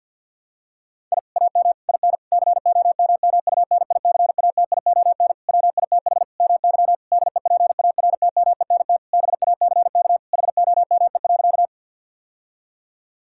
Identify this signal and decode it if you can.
Morse code, heard as a keyed tone, about 52 wpm, 700 Hz. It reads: I AM IN COMMUNICATION WITH MY DEPARTMENT BACK HOME.
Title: Sentences from Top 1000 Words - Review Part 6 - 50wpm